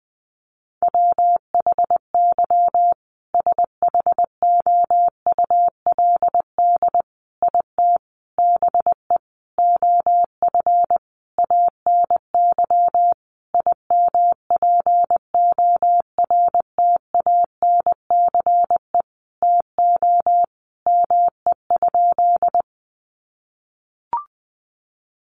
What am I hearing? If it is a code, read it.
Morse code, heard as a keyed tone, about 20 wpm, 700 Hz. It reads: WHY SHOULD IT BE OF ANY IMPORTANCE TO ME?
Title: Sentences from Top 1200 Words 20wpm